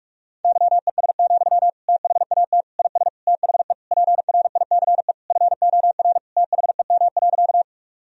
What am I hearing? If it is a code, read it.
Morse code, heard as a keyed tone, about 46 wpm, 700 Hz. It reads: YES, THAT IS THE PRICE FOR THEM.
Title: Sentences from Top 1000 Words - Review Part 4 - 45wpm